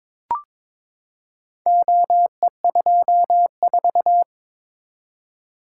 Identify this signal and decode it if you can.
Morse code, heard as a keyed tone, about 22 wpm, 700 Hz. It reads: OE24
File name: practice / 22wpm / Four Letter-Number Combinations - 22wpm